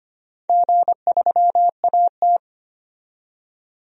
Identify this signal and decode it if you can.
Morse code, heard as a keyed tone, about 25 wpm, 700 Hz. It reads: G3AT
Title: DXCC-100 Call Signs - 4 Characters - 25wpm